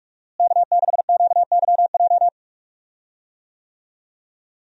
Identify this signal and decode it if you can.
Morse code, heard as a keyed tone, about 46 wpm, 700 Hz. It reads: K6QYJ
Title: USA Call Signs - 5 Characters - Repeated 3x - 45wpm